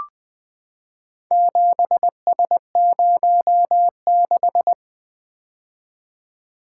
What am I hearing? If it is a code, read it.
Morse code, heard as a keyed tone, about 20 wpm, 700 Hz. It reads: 7S06
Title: Four Letter-Number Combinations - 20wpm